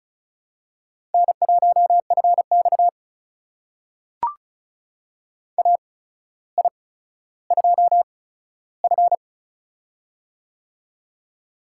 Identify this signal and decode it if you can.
Morse code, heard as a keyed tone, about 36 wpm, spaced as for 15 wpm, 700 Hz. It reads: N1FX AI2F
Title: USA Call Signs - 4 Characters - 8x Character Spacing - 35wpm